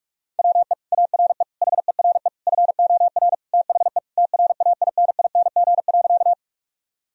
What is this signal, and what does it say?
Morse code, heard as a keyed tone, about 48 wpm, 700 Hz. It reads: WE ARE HERE FOR THE TRAINING.
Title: Sentences from Top 1000 Words - Review Part 4 - 45wpm